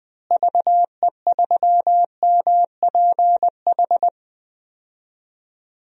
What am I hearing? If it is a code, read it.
Morse code, heard as a keyed tone, about 21 wpm, 700 Hz. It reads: VE3MPH
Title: DXCC-100 Call Signs - 6 Characters - Repeated 3x - 20wpm